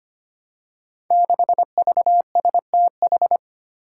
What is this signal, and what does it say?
Morse code, heard as a keyed tone, about 25 wpm, 700 Hz. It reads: 6VSTH
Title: Five Challenging Letter-Number Combinations 25wpm